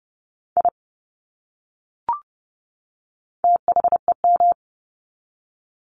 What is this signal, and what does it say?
Morse code, heard as a keyed tone, about 29 wpm, 700 Hz. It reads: I THEM
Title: Top 200 Words 30wpm